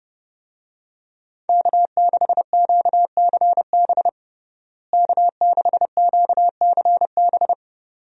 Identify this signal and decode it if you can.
Morse code, heard as a keyed tone, about 30 wpm, 700 Hz. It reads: K6QCB K6QCB
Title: USA Call Signs - 5 Characters - Repeated 2x - 30wpm